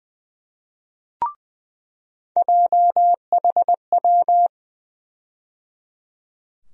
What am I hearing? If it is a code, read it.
Morse code, heard as a keyed tone, about 20 wpm, 700 Hz. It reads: JHW
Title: USA Call Sign Postfixes - 3 Letters 20wpm